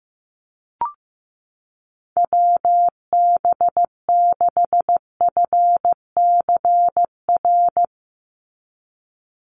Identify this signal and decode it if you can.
Morse code, heard as a keyed tone, about 15 wpm, 700 Hz. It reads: WB6FCR
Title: USA Call Signs 15wpm